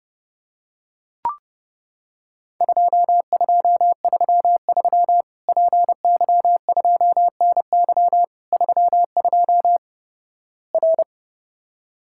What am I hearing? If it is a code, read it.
Morse code, heard as a keyed tone, about 32 wpm, 700 Hz. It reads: 2233 PY2NY 32 R
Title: Worked All Europe DX Contest - QTC Exchange - 30wpm